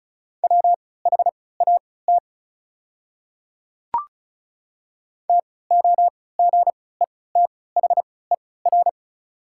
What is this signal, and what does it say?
Morse code, heard as a keyed tone, about 34 wpm, spaced as for 20 wpm, 690 Hz. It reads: WHAT TOGETHER
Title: 3x Character Spacing - Top 300 Words - 35wpm